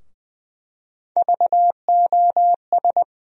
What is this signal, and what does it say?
Morse code, heard as a keyed tone, about 21 wpm, 700 Hz. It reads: VOS
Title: USA Call Sign Postfixes - 3 Letters 20wpm